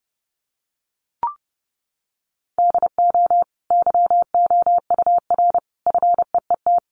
Morse code, heard as a keyed tone, about 30 wpm, 700 Hz. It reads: DO YOUR FEET
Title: Sentences from Top 800 Words 30wpm